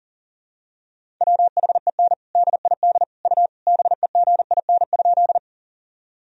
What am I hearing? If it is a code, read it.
Morse code, heard as a keyed tone, about 37 wpm, 700 Hz. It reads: WHEN DID U BEGIN?
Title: Sentences from Top 300 Words and Abbreviations 40wpm